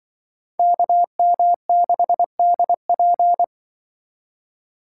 Morse code, heard as a keyed tone, about 24 wpm, 700 Hz. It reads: KM6DP
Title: Speed Racing - USA 2x2 Call Signs 20wpm